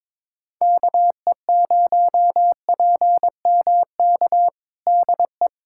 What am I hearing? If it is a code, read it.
Morse code, heard as a keyed tone, about 22 wpm, 700 Hz. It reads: KE0PMK DE
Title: USA Callsign DE Callsign - 6 Characters - 22wpm